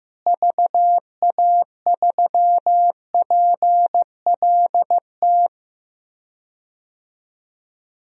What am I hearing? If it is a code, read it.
Morse code, heard as a keyed tone, about 15 wpm, 700 Hz. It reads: VA3PLT
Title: DXCC-100 Call Signs - 6 Characters - Repeated 3x - 15wpm